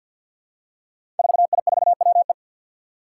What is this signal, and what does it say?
Morse code, heard as a keyed tone, about 52 wpm, 700 Hz. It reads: 4I4WE
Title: Five Challenging Letter-Number Combinations 50wpm